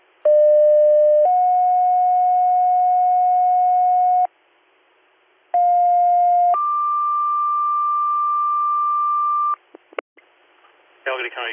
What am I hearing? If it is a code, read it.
Two-tone page, heard as a one-second tone, then a three-second tone, about 600 then 730 Hz, about 710 then 1180 Hz, with speech. It is heard from 0.3 to 9.6 s.